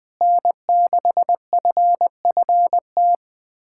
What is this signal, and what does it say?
Morse code, heard as a keyed tone, about 20 wpm, 700 Hz. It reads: N6FFT
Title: USA Callsign DE Callsign - 5 Characters - 20wpm